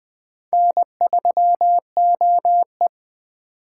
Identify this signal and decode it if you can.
Morse code, heard as a keyed tone, about 21 wpm, 700 Hz. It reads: N3OE